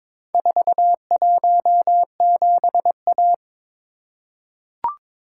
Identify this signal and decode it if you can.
Morse code, heard as a keyed tone, about 23 wpm, 700 Hz. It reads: 417A
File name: practice / 22wpm / Four Letter-Number Combinations - 22wpm